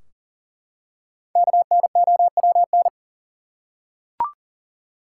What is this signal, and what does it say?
Morse code, heard as a keyed tone, about 40 wpm, 700 Hz. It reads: KNOWN
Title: Five Letter Words - 40wpm